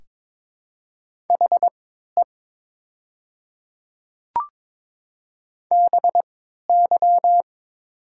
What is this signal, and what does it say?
Morse code, heard as a keyed tone, about 22 wpm, spaced as for 13 wpm, 700 Hz. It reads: HE BY
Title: 3x Character Spacing - Top 200 Words - 22wpm